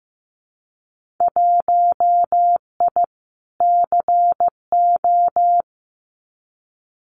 Morse code, heard as a keyed tone, about 15 wpm, 700 Hz. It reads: 1I CO
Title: Winter Field Day Exchanges Encore 15wpm